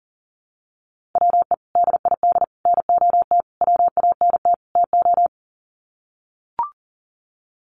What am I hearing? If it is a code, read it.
Morse code, heard as a keyed tone, about 40 wpm, 700 Hz. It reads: WE DID NOT WANT TO
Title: Sets of 5 Words Encore 40wpm